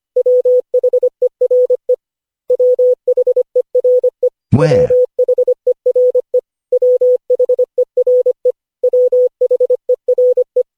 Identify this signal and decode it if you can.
Morse code, heard as a keyed tone, about 25 wpm, 500 Hz. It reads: WHERE WHERE WHERE WHERE WHERE
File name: cw / Session5 cw training